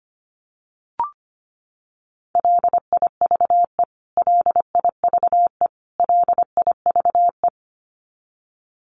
Morse code, heard as a keyed tone, about 25 wpm, 700 Hz. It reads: LIVE LIVE LIVE
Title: Top 200 Words Repeated 25wpm